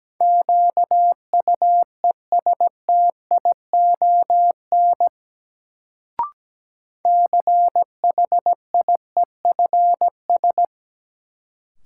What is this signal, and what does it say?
Morse code, heard as a keyed tone, about 17 wpm, 700 Hz. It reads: QUESTION CHIEFS
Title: Sets of 5 Words Encore - New Word Drill - 17wpm